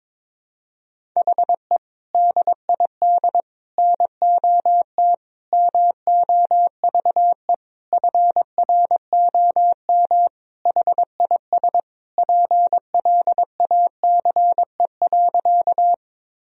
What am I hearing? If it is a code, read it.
Morse code, heard as a keyed tone, about 22 wpm, 700 Hz. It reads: HE DID NOT MOVE FROM HIS PLACE.
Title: Sentences from Top 1000 Words - Review Part 1 - 22wpm